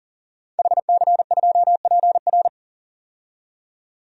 Morse code, heard as a keyed tone, about 40 wpm, 700 Hz. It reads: HC2PR
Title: DXCC-100 Call Signs - 5 Characters - 40wpm